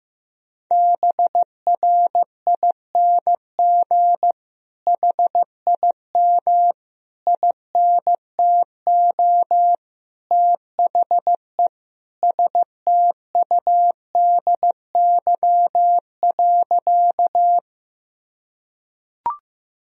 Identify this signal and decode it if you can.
Morse code, heard as a keyed tone, about 15 wpm, 700 Hz. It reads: BRING HIM INTO THE STUDY.
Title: Sentences from Top 1000 Words - Review Part 1 - 15wpm